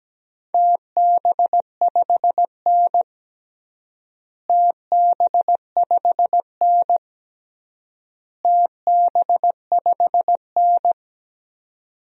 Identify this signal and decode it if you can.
Morse code, heard as a keyed tone, about 17 wpm, 700 Hz. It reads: TB5N TB5N TB5N
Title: DXCC-100 Call Signs - 4 Characters - Repeated 3x - 17wpm